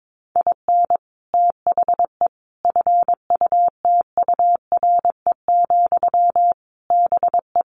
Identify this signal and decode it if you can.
Morse code, heard as a keyed tone, about 22 wpm, 700 Hz. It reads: IN THE FUTURE, BE
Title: Sentences from Top 1300 Words 22wpm